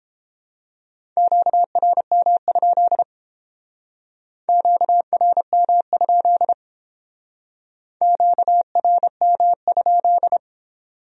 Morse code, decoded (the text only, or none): QRM? QRM? QRM?